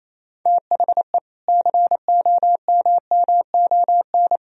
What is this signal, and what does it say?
Morse code, heard as a keyed tone, about 28 wpm, 700 Hz. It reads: THE COMMON
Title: Sentences from Top 1000 Words - Review Part 3 - 28wpm